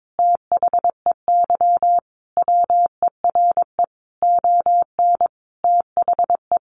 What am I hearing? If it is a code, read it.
Morse code, heard as a keyed tone, about 22 wpm, 700 Hz. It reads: THEY WERE ON THE
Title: Sentences from Top 900 Words 22wpm